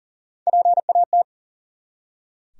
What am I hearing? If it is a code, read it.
Morse code, heard as a keyed tone, about 40 wpm, 700 Hz. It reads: PAT
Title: USA Common Names - Max 3 Letters - 40wpm